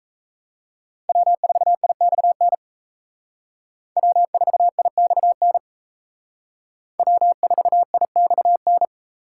Morse code, heard as a keyed tone, about 35 wpm, 700 Hz. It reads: W4IXN W4IXN W4IXN